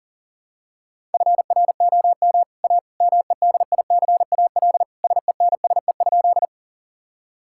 Morse code, heard as a keyed tone, about 40 wpm, 700 Hz. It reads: FROM A MEDICAL SENSE?
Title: Sentences from Top 1000 Words - Review Part 5 - 40wpm